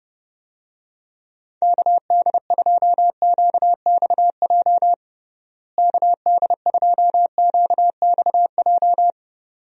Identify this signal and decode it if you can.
Morse code, heard as a keyed tone, about 30 wpm, 700 Hz. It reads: KD2QXJ KD2QXJ
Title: USA Call Signs - 6 Characters - Repeated 2x - 30wpm